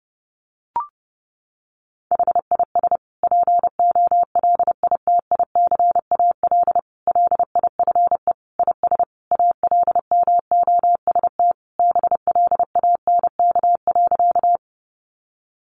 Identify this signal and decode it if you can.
Morse code, heard as a keyed tone, about 30 wpm, 700 Hz. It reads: HIS POLITICAL LIFE IS ALMOST BLANK.
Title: Sentences from Top 1500 Words 30wpm